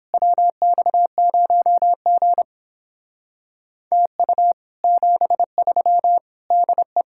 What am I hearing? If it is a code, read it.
Morse code, heard as a keyed tone, about 29 wpm, 700 Hz. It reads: WX0G TU 73 DE